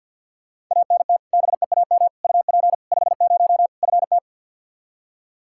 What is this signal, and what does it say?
Morse code, heard as a keyed tone, about 52 wpm, 700 Hz. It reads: ANT BEAM UP 50 FT
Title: Practice QSO - Exchange 2 - Info - 50wpm